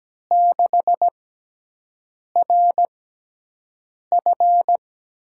Morse code, heard as a keyed tone, about 17 wpm, spaced as for 6 wpm, 700 Hz. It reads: 6RF
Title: USA Call Signs - 5 Characters - 6x Character Spacing - 17wpm